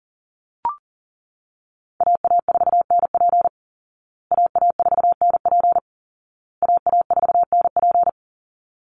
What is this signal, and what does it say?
Morse code, heard as a keyed tone, about 37 wpm, 700 Hz. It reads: AA4NP AA4NP AA4NP
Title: Call Signs Repeated 2x - 40wpm